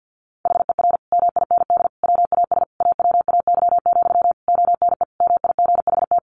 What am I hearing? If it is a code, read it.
Morse code, heard as a keyed tone, about 50 wpm, 700 Hz. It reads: HER MIND WAS AWAY, ONE MIGHT
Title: Sentences from Top 1500 Words 50wpm